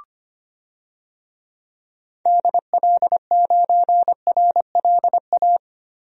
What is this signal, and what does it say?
Morse code, heard as a keyed tone, about 25 wpm, 700 Hz. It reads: DL9RLA
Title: DXCC-100 Call Signs - 6 Characters - Repeated 3x - 25wpm